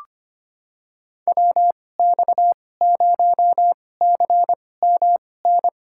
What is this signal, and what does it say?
Morse code, heard as a keyed tone, about 25 wpm, spaced as for 18 wpm, 700 Hz. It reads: WX0CMN
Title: USA Call Signs - 2x Character Spacing - 25wpm